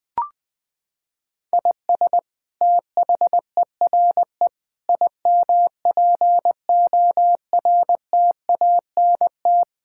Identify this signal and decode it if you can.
Morse code, heard as a keyed tone, about 20 wpm, 700 Hz. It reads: IS THERE IMPORTANT
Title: Sentences from Top 1000 Words - Review Part 6 - 20wpm